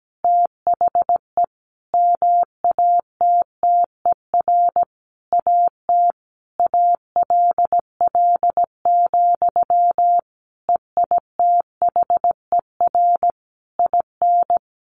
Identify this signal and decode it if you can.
Morse code, heard as a keyed tone, about 17 wpm, 700 Hz. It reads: THE MATTER AT ALL, EITHER IN
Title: Sentences from Top 1200 Words 17wpm